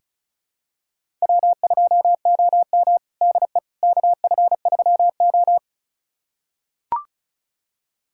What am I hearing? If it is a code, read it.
Morse code, heard as a keyed tone, about 36 wpm, 700 Hz. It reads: W2OM DE KF3O
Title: USA Callsign DE Callsign - 4 Characters - 35wpm